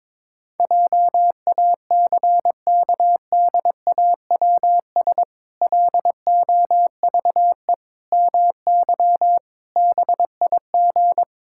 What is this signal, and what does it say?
Morse code, heard as a keyed tone, about 22 wpm, 700 Hz. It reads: JACKDAWS LOVE MY BIG
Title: Pangram Challenge v2 - 22wpm